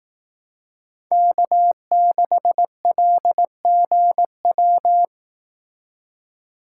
Morse code, heard as a keyed tone, about 18 wpm, 700 Hz. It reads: K6LGW
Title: Speed Racing - USA Call Signs 15wpm